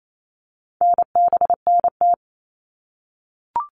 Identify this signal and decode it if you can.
Morse code, heard as a keyed tone, about 28 wpm, 700 Hz. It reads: NBNT
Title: All Cut Numbers - 4-Digit 28wpm